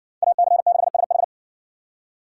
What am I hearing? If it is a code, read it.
Morse code, heard as a keyed tone, about 60 wpm, 700 Hz. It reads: AC6SH